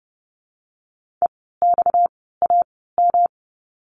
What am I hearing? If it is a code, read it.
Morse code, heard as a keyed tone, about 30 wpm, spaced as for 17 wpm, 700 Hz. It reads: EXAM
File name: course / 30:10wpm / 230 Lesson 37f - Letter x - Words 30wpm